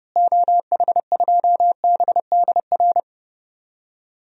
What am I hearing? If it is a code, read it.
Morse code, heard as a keyed tone, about 32 wpm, 700 Hz. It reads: OH2BDR
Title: DXCC-100 Call Signs - 6 Characters - Repeated 2x - 30wpm